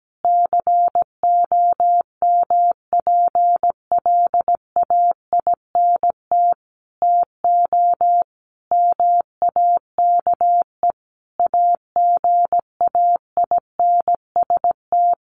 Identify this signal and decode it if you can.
Morse code, heard as a keyed tone, about 17 wpm, 700 Hz. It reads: COMPLAINT TO MAKE AGAINST
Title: Sentences from Top 1400 Words 17wpm